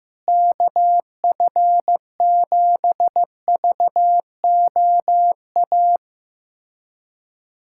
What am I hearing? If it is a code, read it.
Morse code, heard as a keyed tone, about 15 wpm, 700 Hz. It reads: KF7VOA
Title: USA Callsign DE Callsign - 6 Characters - 15wpm